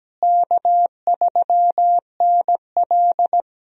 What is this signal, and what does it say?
Morse code, heard as a keyed tone, about 17 wpm, 700 Hz. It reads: K3NL
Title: USA Call Signs - 4 Characters - Repeated 3x - 17wpm